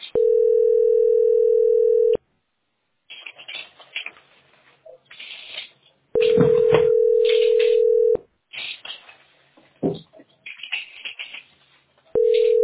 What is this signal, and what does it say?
Ringback tone (North America), 440 + 480 Hz.